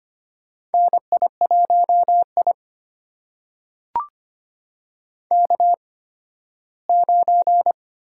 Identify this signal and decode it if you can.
Morse code, heard as a keyed tone, about 25 wpm, spaced as for 15 wpm, 700 Hz. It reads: NI1I K9
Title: USA Call Signs - 4 Characters - 8x Character Spacing - 25wpm